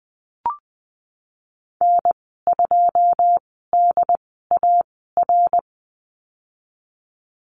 Morse code, heard as a keyed tone, about 20 wpm, spaced as for 15 wpm, 700 Hz. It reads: N2DAR
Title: 088 Lesson 15g - Number 5 - Call Signs 20wpm